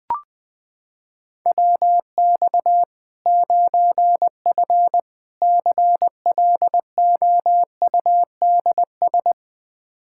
Morse code, heard as a keyed tone, about 20 wpm, 700 Hz. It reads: WX 9F CLOUDS
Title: Practice QSO - Exchange 2 - Info - 20wpm